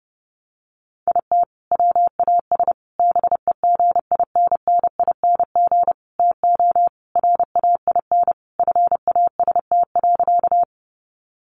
Morse code, heard as a keyed tone, about 30 wpm, 700 Hz. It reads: IT WAS BEGINNING TO RAIN FAST.
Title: Sentences from Top 800 Words 30wpm